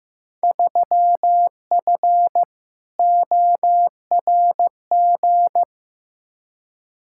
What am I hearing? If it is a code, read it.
Morse code, heard as a keyed tone, about 15 wpm, 700 Hz. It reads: 3F ORG
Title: Field Day Exchanges - 15wpm